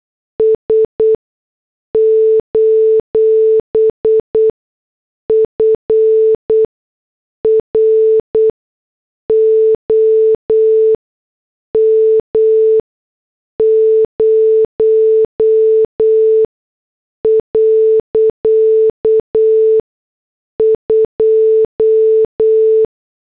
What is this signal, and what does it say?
Morse code, heard as a keyed tone, about 8 wpm, spaced as for 6 wpm, 440 Hz. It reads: S: FROM 0.2